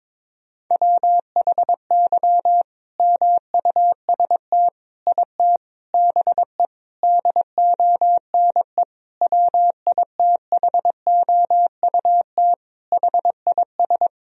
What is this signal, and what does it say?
Morse code, heard as a keyed tone, about 22 wpm, 700 Hz. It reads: WHY MUST IT BE DONE WITHOUT HIS
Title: Sentences from Top 1000 Words - Review Part 3 - 22wpm